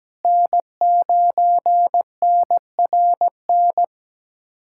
Morse code, heard as a keyed tone, about 17 wpm, 700 Hz. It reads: N9NRN
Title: USA Callsign DE Callsign - 5 Characters - 17wpm